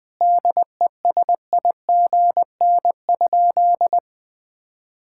Morse code, heard as a keyed tone, about 20 wpm, 700 Hz. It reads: DESIGN?
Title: Sentences with 5-Word Phrases - 20wpm